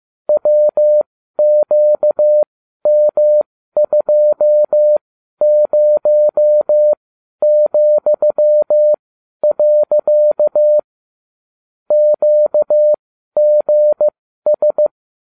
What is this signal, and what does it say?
Morse code, heard as a keyed tone, about 15 wpm, spaced as for 12 wpm, 600 Hz. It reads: WQM20,. QGS